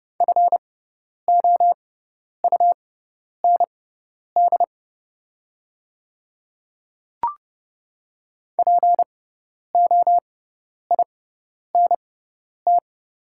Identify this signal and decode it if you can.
Morse code, heard as a keyed tone, about 30 wpm, spaced as for 10 wpm, 700 Hz. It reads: FOUND POINT
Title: 6x Character Spacing - Top 300 Words - 30wpm